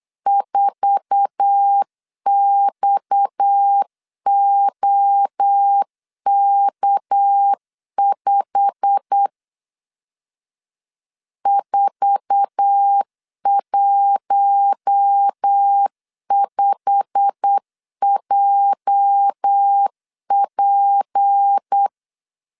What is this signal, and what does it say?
Morse code, heard as a keyed tone, about 8 wpm, 800 Hz. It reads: 4XOK5 415JP
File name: LEZIONI / 8 YJ45 / 8 YJ45 8wpm 3214